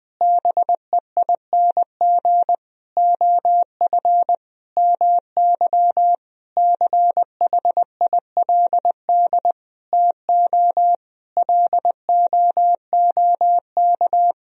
Morse code, heard as a keyed tone, about 20 wpm, 700 Hz. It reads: BEING OF MY CHILD TO LOOK
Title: Sentences from Top 1000 Words - Review Part 3 - 20wpm